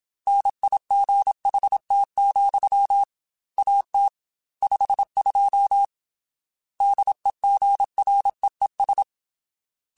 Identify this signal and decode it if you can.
Morse code, heard as a keyed tone, about 27 wpm, 800 Hz. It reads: NIGHT, AT 52 DEGREES
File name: Weather Sets / Weather Sets-mp3s / Weather 4 17 @25